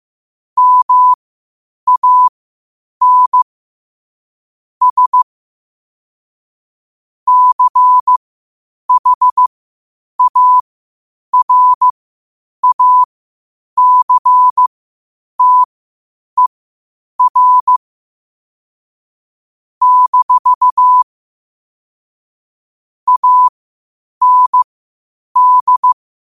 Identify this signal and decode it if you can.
Morse code, heard as a keyed tone, about 15 wpm, spaced as for 8 wpm, 1000 Hz. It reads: MAN S CHARACTER - AND
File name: qotdfiles / 8wpm